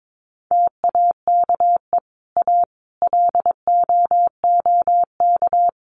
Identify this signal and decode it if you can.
Morse code, heard as a keyed tone, about 22 wpm, 700 Hz. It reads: TAKE A LOOK
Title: Sets of 5 Words Encore2 22wpm